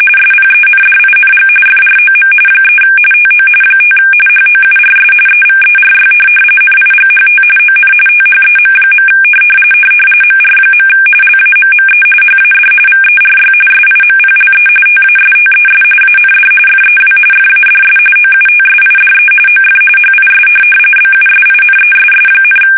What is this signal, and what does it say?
RTTY, about 45.45 baud. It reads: SPENDING FROM 3 TO 5 HOURS EVERY NIGHT ON THEM. MR. LAMME HAD VERY SIMPLE TASTES AND FEW FADS OUTSIDE OF HIS BUSINESS,
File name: fsk850 / benjamin-g-lamme